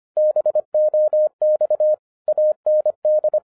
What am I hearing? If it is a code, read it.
Morse code, heard as a keyed tone, about 25 wpm, 610 Hz. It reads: BOX AND
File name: FIRST FISTS CONTACT-25wpm